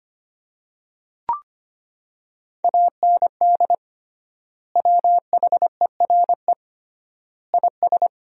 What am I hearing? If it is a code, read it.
Morse code, heard as a keyed tone, about 25 wpm, 700 Hz. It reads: AND WHERE IS
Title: 3x Word-Spacing - Sentences from Top 200 Words - 25wpm